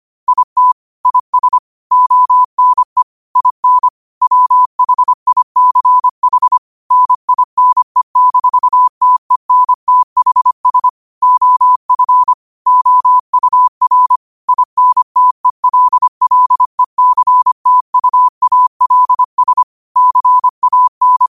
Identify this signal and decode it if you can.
Morse code, heard as a keyed tone, about 25 wpm, 1000 Hz. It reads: IT IS ONE IN WHICH NINE-TENTHS OF OUR INTELLECTUALS CAN
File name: qotdfiles / 25wpm